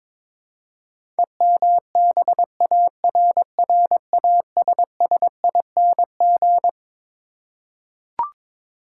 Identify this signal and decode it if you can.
Morse code, heard as a keyed tone, about 22 wpm, 700 Hz. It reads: EMBARRASSING
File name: Twelve Letter Words - 22wpm